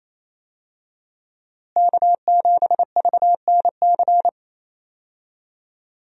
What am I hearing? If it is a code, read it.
Morse code, heard as a keyed tone, about 28 wpm, 700 Hz. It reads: K7VNC
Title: USA Call Signs - 5 Characters - Repeated 3x - 28wpm